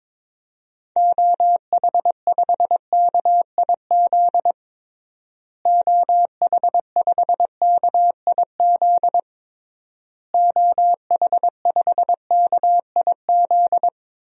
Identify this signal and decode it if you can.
Morse code, heard as a keyed tone, about 22 wpm, 700 Hz. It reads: OH5KIZ OH5KIZ OH5KIZ
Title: DXCC-100 Call Signs - 6 Characters - Repeated 3x - 22wpm